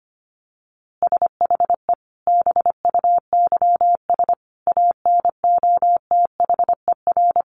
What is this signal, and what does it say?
Morse code, heard as a keyed tone, about 25 wpm, 700 Hz. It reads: SHE BUYS ANOTHER
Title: Easy Phrases 25wpm